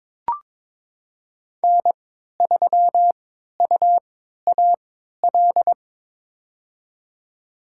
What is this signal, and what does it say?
Morse code, heard as a keyed tone, about 22 wpm, spaced as for 12 wpm, 700 Hz. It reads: N3UAL
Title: USA Call Signs - 5 Characters - 3x Character Spacing - 22wpm